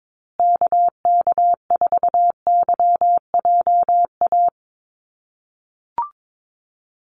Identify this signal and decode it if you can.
Morse code, heard as a keyed tone, about 22 wpm, 700 Hz. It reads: KK4YJA
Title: USA Call Signs Repeated 2x 22wpm